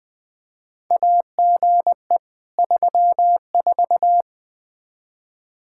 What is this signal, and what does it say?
Morse code, heard as a keyed tone, about 21 wpm, 700 Hz. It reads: AGE 34
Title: Practice QSO - Exchange 3 - More Info - 20wpm